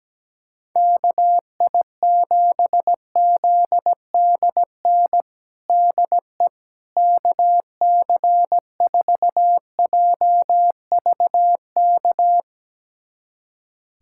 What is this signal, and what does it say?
Morse code, heard as a keyed tone, about 17 wpm, 700 Hz. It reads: KI7ZDN DE KC4JVK
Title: USA Callsign DE Callsign - 6 Characters - 17wpm